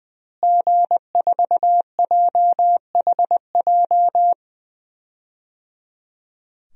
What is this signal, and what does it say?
Morse code, heard as a keyed tone, about 19 wpm, 700 Hz. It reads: G4JHJ
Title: DXCC-100 Call Signs - 5 Characters - Repeated 3x - 20wpm